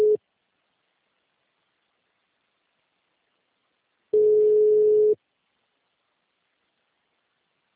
Ringback tone (Europe, 425 Hz).
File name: out- -2004- - - .6